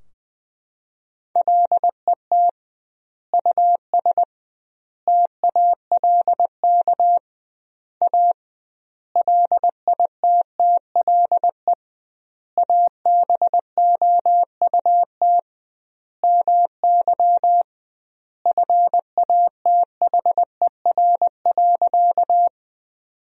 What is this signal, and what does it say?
Morse code, heard as a keyed tone, about 20 wpm, 700 Hz. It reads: LET US TALK A LITTLE ABOUT MY FATHER.
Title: 2x Word-Spacing - Sentences from Top 400 Words - 20wpm